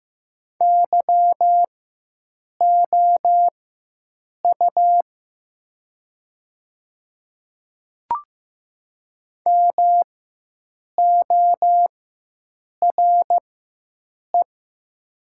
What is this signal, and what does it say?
Morse code, heard as a keyed tone, about 15 wpm, spaced as for 7 wpm, 700 Hz. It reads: YOU MORE
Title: 4x Character Spacing - Top 100 Words - 15wpm